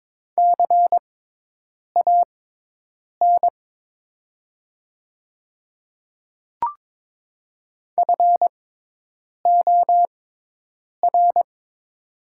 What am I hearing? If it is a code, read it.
Morse code, heard as a keyed tone, about 22 wpm, spaced as for 8 wpm, 700 Hz. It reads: CAN FOR
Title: 6x Character Spacing - Top 100 Words - 22wpm